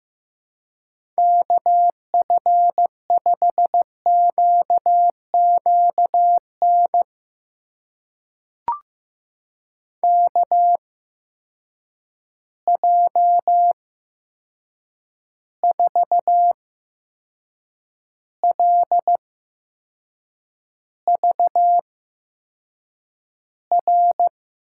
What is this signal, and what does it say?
Morse code, heard as a keyed tone, about 15 wpm, spaced as for 6 wpm, 700 Hz. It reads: KF5QQN KJ4LVR
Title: USA Call Signs - 6 Characters - 8x Character Spacing - 15wpm